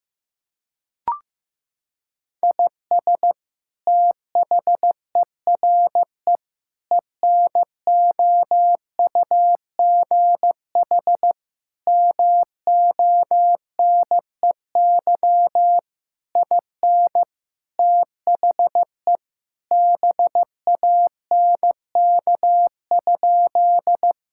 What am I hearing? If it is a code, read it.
Morse code, heard as a keyed tone, about 16 wpm, 700 Hz. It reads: IS THERE ENOUGH MONEY IN THE BANK?
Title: Sentences from Top 1000 Words - Review Part 6 - 15wpm